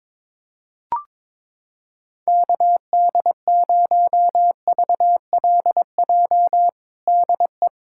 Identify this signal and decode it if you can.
Morse code, heard as a keyed tone, about 22 wpm, 700 Hz. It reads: KD0VLJ DE
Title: USA Callsign DE Callsign - 6 Characters - 22wpm